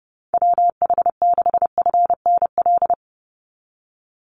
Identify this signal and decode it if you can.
Morse code, heard as a keyed tone, about 30 wpm, 700 Hz. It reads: WH6FNL
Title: USA Call Signs 30wpm